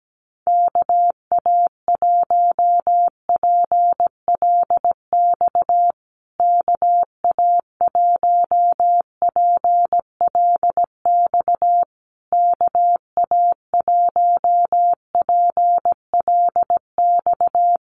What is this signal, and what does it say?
Morse code, heard as a keyed tone, about 17 wpm, 700 Hz. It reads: KA1PLX KA1PLX KA1PLX
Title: USA Call Signs Repeated 3x 17wpm